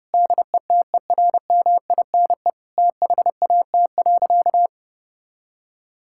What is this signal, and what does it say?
Morse code, heard as a keyed tone, about 30 wpm, 700 Hz. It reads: DETERMINE THAT.
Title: Sentences from Top 1000 Words - Review Part 5 - 30wpm